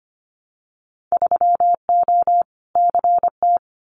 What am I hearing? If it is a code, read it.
Morse code, heard as a keyed tone, about 25 wpm, 690 Hz. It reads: 3O CT
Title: Winter Field Day Exchanges 25wpm